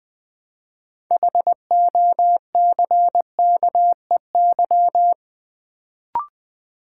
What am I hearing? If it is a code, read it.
Morse code, heard as a keyed tone, about 20 wpm, 700 Hz. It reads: HOCKEY
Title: Build A Word - 3 Letters At A Time - 6-Letter Words - 20wpm